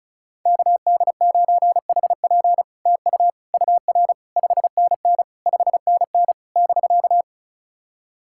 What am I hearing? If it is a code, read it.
Morse code, heard as a keyed tone, about 36 wpm, 700 Hz. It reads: KD9HP TU UR 5NN 5NN <BK>
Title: SOTA QSO Practice 35wpm